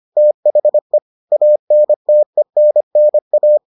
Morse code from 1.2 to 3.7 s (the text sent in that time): ANTENNA